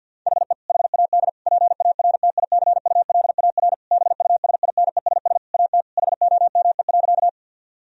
Morse code, heard as a keyed tone, about 50 wpm, 700 Hz. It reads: HE HAD PARTICULAR BUSINESS AT HOME.
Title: Sentences from Top 1000 Words - Review Part 4 - 50wpm